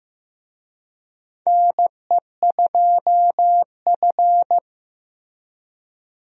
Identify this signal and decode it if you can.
Morse code, heard as a keyed tone, about 16 wpm, 700 Hz. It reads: NE2F
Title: USA Call Signs - 4 Characters - Repeated 3x - 15wpm